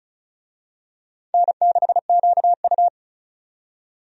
Morse code, heard as a keyed tone, about 34 wpm, 700 Hz. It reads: N6QU
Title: USA Call Signs - 4 Characters - 35wpm